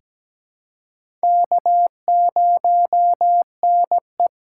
Morse code, heard as a keyed tone, about 17 wpm, 700 Hz. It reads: K0NE
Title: USA Call Signs - 4 Characters - 3x Character Spacing - 17wpm